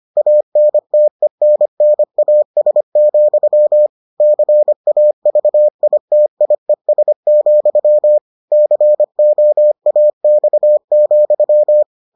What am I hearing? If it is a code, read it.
Morse code, heard as a keyed tone, about 25 wpm, 600 Hz. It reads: ANTENNAS, CAVITIES, COAX,